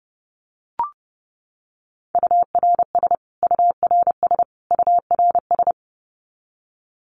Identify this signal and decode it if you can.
Morse code, heard as a keyed tone, about 30 wpm, 700 Hz. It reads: URS URS URS
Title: Abbreviations Repeated 3x - 30wpm